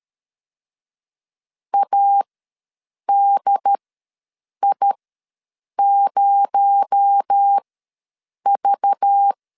Morse code, heard as a keyed tone, about 13 wpm, spaced as for 7 wpm, 800 Hz. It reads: ADI0V
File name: LEZIONI / 8 YJ45 / 8 Y ws 12wpm 3409